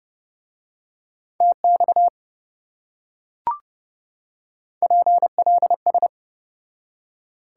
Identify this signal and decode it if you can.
Morse code, heard as a keyed tone, about 27 wpm, 700 Hz. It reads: TX PLS